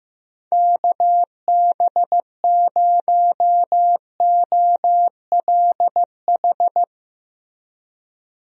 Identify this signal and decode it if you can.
Morse code, heard as a keyed tone, about 15 wpm, 700 Hz. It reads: KB0OLH
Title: USA Call Signs - 6 Characters - 15wpm